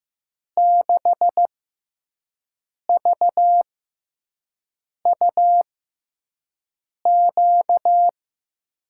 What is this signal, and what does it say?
Morse code, heard as a keyed tone, about 15 wpm, spaced as for 5 wpm, 700 Hz. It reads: 6VUQ
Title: USA Call Signs - 5 Characters - 6x Character Spacing - 15wpm